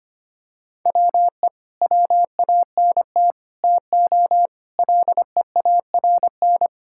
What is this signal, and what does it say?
Morse code, heard as a keyed tone, about 25 wpm, 700 Hz. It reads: WE WANT TO LEARN
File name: practice / 25wpm / Sentences with 3-Word Phrases - 25wpm